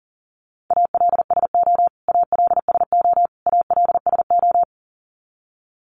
Morse code, heard as a keyed tone, about 37 wpm, 700 Hz. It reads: ALSO ALSO ALSO
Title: Top 200 Words Repeated 40wpm